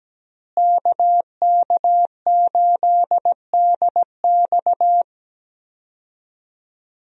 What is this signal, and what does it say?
Morse code, heard as a keyed tone, about 17 wpm, 700 Hz. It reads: KK8DX
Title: USA Call Signs - 5 Characters - Repeated 2x - 17wpm